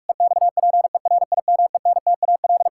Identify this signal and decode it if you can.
Morse code, heard as a keyed tone, about 46 wpm, 700 Hz. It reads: EXPERIMENTAL
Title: Pangram Challenge v2 - 45wpm